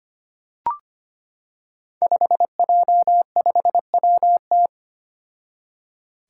Morse code, heard as a keyed tone, about 25 wpm, 700 Hz. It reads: 5J5WT
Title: Five Challenging Letter-Number Combinations 25wpm